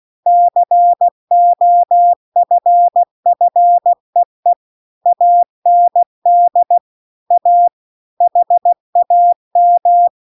Morse code, heard as a keyed tone, about 16 wpm, 700 Hz. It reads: COFFEE AND A HAM